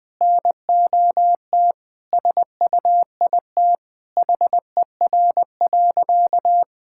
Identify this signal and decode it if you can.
Morse code, heard as a keyed tone, about 21 wpm, 700 Hz. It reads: NOT SUIT HER.
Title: Sentences from Top 1600 Words 20wpm